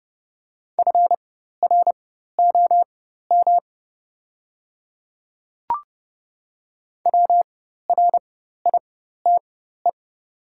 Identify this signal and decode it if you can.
Morse code, heard as a keyed tone, about 30 wpm, spaced as for 14 wpm, 700 Hz. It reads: FROM WRITE
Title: 4x Character Spacing - Top 200 Words - 30wpm